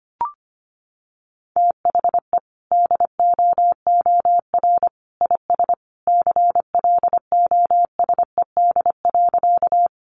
Morse code, heard as a keyed tone, about 25 wpm, 700 Hz. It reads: THE DOOR IS CLOSED.
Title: Sentences from Top 1400 Words 25wpm